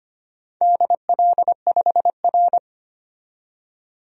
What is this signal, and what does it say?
Morse code, heard as a keyed tone, about 25 wpm, 700 Hz. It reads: DL5R
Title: DXCC-100 Call Signs - 4 Characters - 25wpm